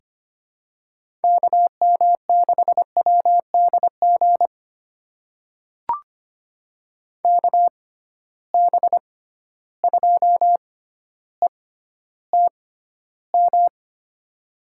Morse code, heard as a keyed tone, about 25 wpm, spaced as for 13 wpm, 700 Hz. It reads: KM6WDG KB2ETM
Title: USA Call Signs - 6 Characters - 6x Character Spacing - 25wpm